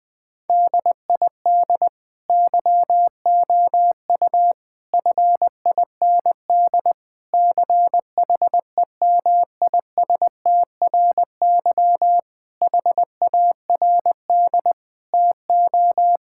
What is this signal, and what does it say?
Morse code, heard as a keyed tone, about 20 wpm, 700 Hz. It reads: DID YOU FIND CHEMISTRY HARD TO
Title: Sentences from Top 1000 Words - Review Part 6 - 20wpm